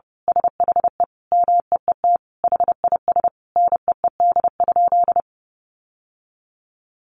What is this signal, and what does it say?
Morse code, heard as a keyed tone, about 30 wpm, 700 Hz. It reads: SHE MEET HIS NEED?
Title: Sentences from Top 1100 Words 30wpm